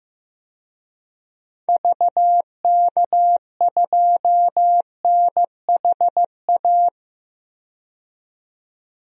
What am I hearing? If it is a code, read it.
Morse code, heard as a keyed tone, about 15 wpm, 700 Hz. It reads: VK2NHA
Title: DXCC-100 Call Signs - 6 Characters - 15wpm